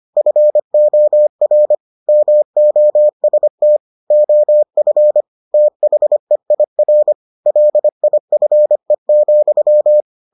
Morse code, heard as a keyed tone, about 25 wpm, 600 Hz. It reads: FOR MOST OF THEIR LIFE,